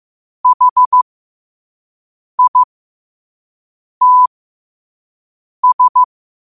Morse code, heard as a keyed tone, about 15 wpm, spaced as for 5 wpm, 1000 Hz. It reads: HITS
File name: qotdfiles / 5wpm